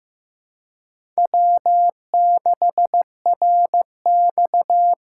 Morse code, heard as a keyed tone, about 15 wpm, 700 Hz. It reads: W6RX
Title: USA Call Signs - 3x Character Spacing - 15wpm